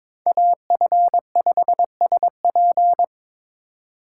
Morse code, heard as a keyed tone, about 26 wpm, 700 Hz. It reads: AF5SP
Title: Speed Racing - USA 2x2 Call Signs 22wpm